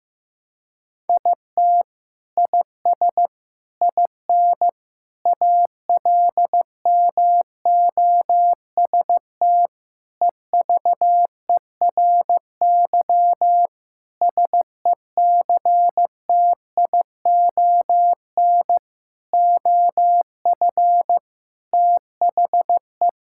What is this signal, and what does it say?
Morse code, heard as a keyed tone, about 15 wpm, 700 Hz. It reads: IT IS IN ALMOST EVERY SECTION OF THE
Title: Sentences from Top 1000 Words - Review Part 5 - 15wpm